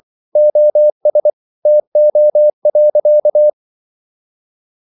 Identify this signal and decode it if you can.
Morse code, heard as a keyed tone, about 24 wpm, 600 Hz. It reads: OS TO.
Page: Ferie 24 wpm